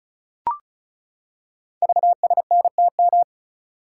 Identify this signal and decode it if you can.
Morse code, heard as a keyed tone, about 36 wpm, 700 Hz. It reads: VSNTM
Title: Code Groups - 5 Letter:Numbers - 35wpm